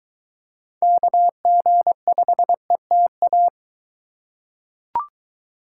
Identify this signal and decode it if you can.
Morse code, heard as a keyed tone, about 22 wpm, 700 Hz. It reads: KG5ETA